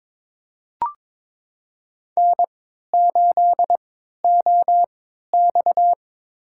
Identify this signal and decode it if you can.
Morse code, heard as a keyed tone, about 23 wpm, spaced as for 13 wpm, 700 Hz. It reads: N8OX
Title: USA Call Signs - 4 Characters - 3x Character Spacing - 22wpm